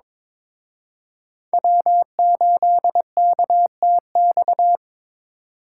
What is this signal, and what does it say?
Morse code, heard as a keyed tone, about 26 wpm, 700 Hz. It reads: W8KTX
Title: Speed Racing - USA 1x3 Call Signs 22wpm